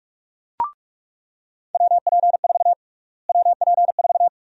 Morse code, heard as a keyed tone, about 46 wpm, 700 Hz. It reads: WP4 WP4
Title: USA Call Sign Prefixes - Repeated 2x - 45wpm